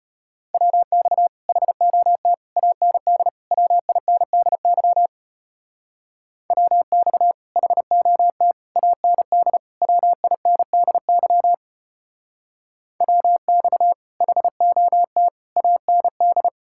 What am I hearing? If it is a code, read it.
Morse code, heard as a keyed tone, about 31 wpm, 700 Hz. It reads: WX HOT AND WINDY WX HOT AND WINDY WX HOT AND
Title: Speed-Racing Common QSO Phrases - 25wpm